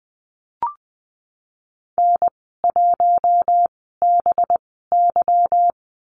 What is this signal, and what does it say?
Morse code, heard as a keyed tone, about 20 wpm, 700 Hz. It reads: N 1 B Y
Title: 200 Lesson 32g - Letter k - Call Signs 20wpm